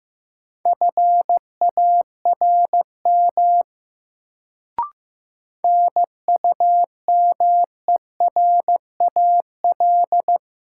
Morse code, heard as a keyed tone, about 15 wpm, 700 Hz. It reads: FARM NUMERAL
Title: Sentences from Top 1000 Words - Review Part 2 - New Word Drill - 15wpm